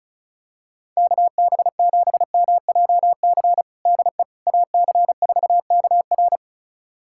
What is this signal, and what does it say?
Morse code, heard as a keyed tone, about 34 wpm, 700 Hz. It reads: KB7MJC DE AC4KR
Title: USA Callsign DE Callsign 35wpm